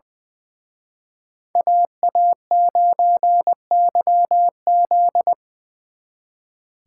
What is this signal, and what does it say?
Morse code, heard as a keyed tone, about 24 wpm, 700 Hz. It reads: AA9YZ
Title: Speed Racing - USA Call Signs 20wpm